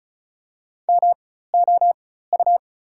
Morse code, heard as a keyed tone, about 34 wpm, spaced as for 16 wpm, 700 Hz. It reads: MOU
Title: 4x Character Spacing - Top 300 Words - 35wpm